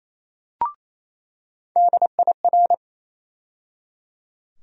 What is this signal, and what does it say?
Morse code, heard as a keyed tone, about 28 wpm, 700 Hz. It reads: DIR
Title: USA Call Sign Postfixes - 3 Letters 28wpm